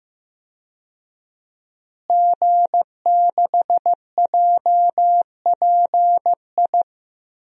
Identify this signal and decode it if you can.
Morse code, heard as a keyed tone, about 15 wpm, 700 Hz. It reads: G6JPI
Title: DXCC-100 Call Signs - 5 Characters - Repeated 3x - 15wpm